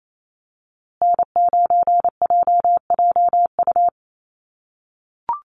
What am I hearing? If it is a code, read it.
Morse code, heard as a keyed tone, about 28 wpm, 700 Hz. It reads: N9JJU
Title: USA Call Signs Repeated 3x 28wpm